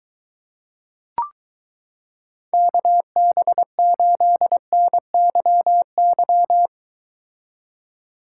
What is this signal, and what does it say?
Morse code, heard as a keyed tone, about 23 wpm, 700 Hz. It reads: KB8NYY